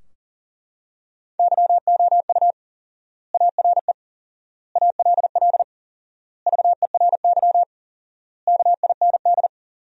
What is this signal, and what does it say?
Morse code, heard as a keyed tone, about 40 wpm, 700 Hz. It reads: YOU ARE ALL VERY KIND
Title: 4x Word-Spacing - Sentences from Top 200 Words - 40wpm